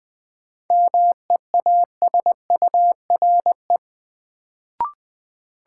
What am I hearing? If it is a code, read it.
Morse code, heard as a keyed tone, about 20 wpm, 700 Hz. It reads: MEASURE
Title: Sentences from Top 1000 Words - Review Part 1 - New Word Drill - 20wpm